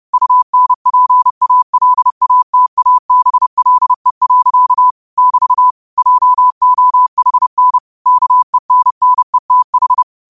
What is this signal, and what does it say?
Morse code, heard as a keyed tone, about 30 wpm, 1000 Hz. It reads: UNPALATABLE. <BT> JOHN KENNETH